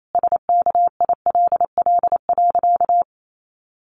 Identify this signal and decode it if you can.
Morse code, heard as a keyed tone, about 28 wpm, 700 Hz. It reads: SKILL.
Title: Sentences from Top 1200 Words 28wpm